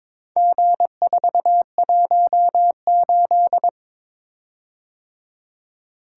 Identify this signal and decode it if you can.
Morse code, heard as a keyed tone, about 22 wpm, 700 Hz. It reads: G418
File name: practice / 22wpm / Four Letter-Number Combinations - 22wpm